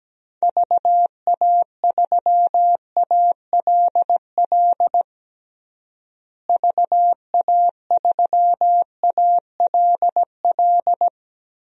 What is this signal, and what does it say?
Morse code, heard as a keyed tone, about 17 wpm, 700 Hz. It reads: VA3ALL VA3ALL
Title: DXCC-100 Call Signs - 6 Characters - Repeated 2x - 17wpm